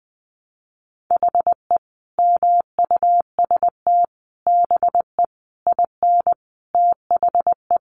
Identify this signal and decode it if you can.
Morse code, heard as a keyed tone, about 20 wpm, 700 Hz. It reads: HE MUST BE IN THE
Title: Sentences from Top 500 Words 20wpm